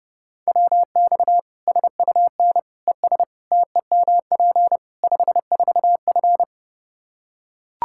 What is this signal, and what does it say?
Morse code, heard as a keyed tone, about 30 wpm, 700 Hz. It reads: WX SUN ES TEMP 54F
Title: Practice QSO - Exchange 2 - Info - 30wpm